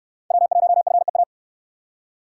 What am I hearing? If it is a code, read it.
Morse code, heard as a keyed tone, about 66 wpm, 700 Hz. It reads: W1FEA